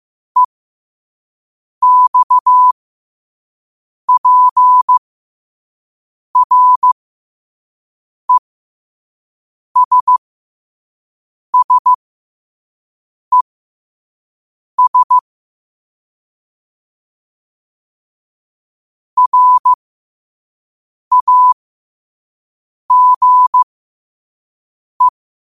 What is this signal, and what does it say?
Morse code, heard as a keyed tone, about 15 wpm, spaced as for 5 wpm, 1000 Hz. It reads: EXPRESSES RAGE
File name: qotdfiles / 5wpm